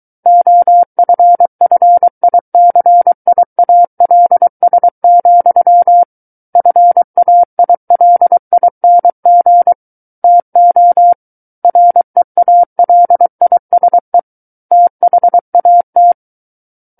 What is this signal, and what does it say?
Morse code, heard as a keyed tone, about 23 wpm, 700 Hz. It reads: OFFICIALS, FAILING TO REALISE THAT
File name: Book One / 14-Chapter 14 - 23 WPM